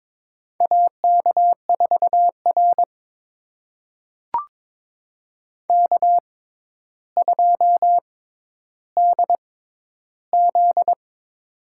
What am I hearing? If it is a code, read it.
Morse code, heard as a keyed tone, about 22 wpm, spaced as for 11 wpm, 700 Hz. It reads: AK4R K2DZ
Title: USA Call Signs - 4 Characters - 6x Character Spacing - 22wpm